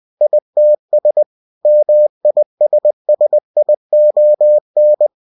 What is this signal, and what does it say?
Morse code, heard as a keyed tone, about 20 wpm, 600 Hz. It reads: ITS MISSION
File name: cw-books / winnie-the-pooh / winnie-20wpm-page-0121